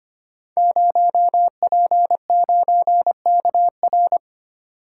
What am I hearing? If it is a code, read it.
Morse code, heard as a keyed tone, about 25 wpm, 700 Hz. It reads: 0P9KR
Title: Code Groups - 5 Letter:Numbers - 25wpm